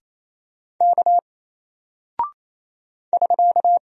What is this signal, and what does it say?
Morse code, heard as a keyed tone, about 28 wpm, 700 Hz. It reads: K <SK>
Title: Single Character (Letter:Number:Punctuation:Prosign) Alt 28wpm